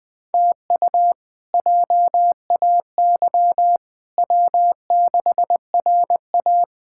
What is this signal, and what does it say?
Morse code, heard as a keyed tone, about 21 wpm, 700 Hz. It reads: TU JAY W6RA
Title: K1USN SST QSO Practice - 20wpm